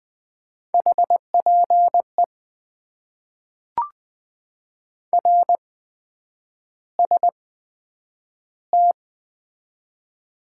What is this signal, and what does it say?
Morse code, heard as a keyed tone, about 20 wpm, spaced as for 9 wpm, 700 Hz. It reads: HPE RST
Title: Top 90 QSO Elements - 8x Character Spacing - 20wpm